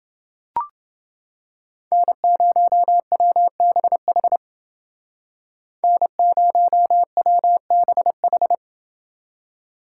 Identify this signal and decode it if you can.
Morse code, heard as a keyed tone, about 29 wpm, 700 Hz. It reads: N0WBH N0WBH